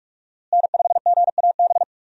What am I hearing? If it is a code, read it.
Morse code, heard as a keyed tone, about 42 wpm, 700 Hz. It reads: N5GAB
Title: USA Call Signs - 5 Characters - Repeated 2x - 45wpm